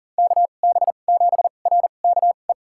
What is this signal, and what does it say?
Morse code, heard as a keyed tone, about 40 wpm, spaced as for 29 wpm, 700 Hz. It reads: KB7RKE
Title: USA Call Signs - 6 Characters - 2x Character Spacing - 40wpm